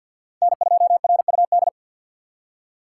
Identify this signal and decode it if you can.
Morse code, heard as a keyed tone, about 53 wpm, 700 Hz. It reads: N 2 R U D
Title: USA Call Signs - 5 Characters - 6x Character Spacing - 50wpm